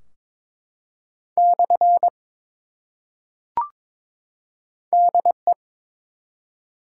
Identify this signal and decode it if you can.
Morse code, heard as a keyed tone, about 22 wpm, 700 Hz. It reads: / DE
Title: Top 180 QSO Elements 22wpm